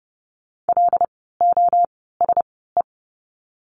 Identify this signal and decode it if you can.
Morse code, heard as a keyed tone, about 30 wpm, spaced as for 17 wpm, 700 Hz. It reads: LOSE
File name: Lesson 16f - Letter c - Words 30wpm